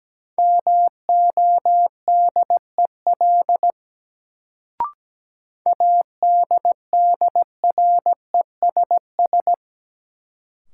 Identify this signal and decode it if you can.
Morse code, heard as a keyed tone, about 17 wpm, 700 Hz. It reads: MODEL ADDRESS
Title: Sentences from Top 1000 Words - New Word Drill - 17wpm